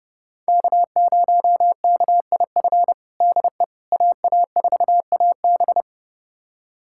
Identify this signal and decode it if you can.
Morse code, heard as a keyed tone, about 32 wpm, 700 Hz. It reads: K0KIF DE AA4AB
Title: USA Callsign DE Callsign - 5 Characters - 30wpm